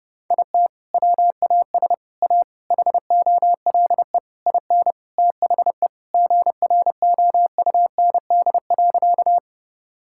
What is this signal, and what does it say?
Morse code, heard as a keyed tone, about 30 wpm, 700 Hz. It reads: IT WAS A HOLE IN THE GROUND.
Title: Sentences from Top 1600 Words 30wpm